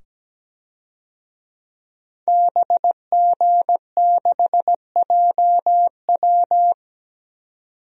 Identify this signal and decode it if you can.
Morse code, heard as a keyed tone, about 17 wpm, 700 Hz. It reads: BG6JW
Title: DXCC-100 Call Signs - 5 Characters - 17wpm